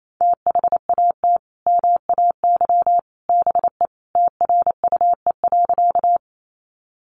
Morse code, heard as a keyed tone, about 28 wpm, 700 Hz. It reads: THAT MAY BE TRUE.
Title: Sentences from Top 400 Words 28wpm